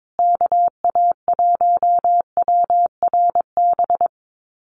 Morse code, heard as a keyed tone, about 22 wpm, 700 Hz. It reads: KA1WRB
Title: USA Call Signs Repeated 3x 22wpm